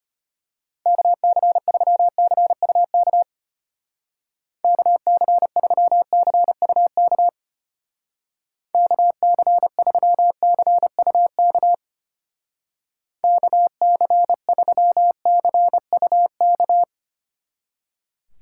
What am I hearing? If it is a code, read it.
Morse code, heard as a keyed tone, about 30 wpm, 700 Hz. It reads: KC3CUK KC3CUK KC3CUK KC3CUK
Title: Speed Racing - USA 2x3 Call Signs 25wpm